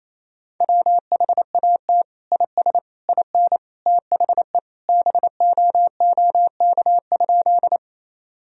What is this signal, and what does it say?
Morse code, heard as a keyed tone, about 28 wpm, 700 Hz. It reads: WHAT IS IN THE BOOK?
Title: Sentences from Top 1000 Words - Review Part 1 - 28wpm